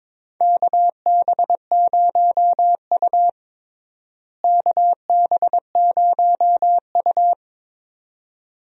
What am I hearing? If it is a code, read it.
Morse code, heard as a keyed tone, about 22 wpm, 700 Hz. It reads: KB0U KB0U
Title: USA Call Signs - 4 Characters - Repeated 2x - 22wpm